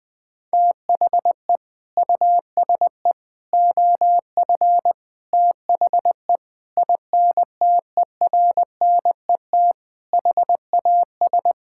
Morse code, heard as a keyed tone, about 20 wpm, 700 Hz. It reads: THE USE OF THE INTERNET HAS
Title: Sentences with 4-Word Phrases - 20wpm